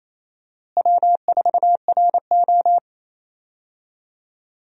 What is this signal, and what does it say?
Morse code, heard as a keyed tone, about 28 wpm, 700 Hz. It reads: W4RO
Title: USA Call Signs - 4 Characters - Repeated 2x - 28wpm